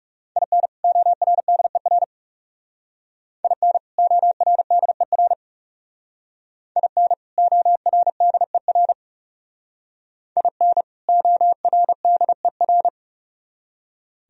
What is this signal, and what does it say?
Morse code, heard as a keyed tone, about 34 wpm, 700 Hz. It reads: IN ORDER IN ORDER IN ORDER IN ORDER
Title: Speed-Racing - Sets of 2 Words 30wpm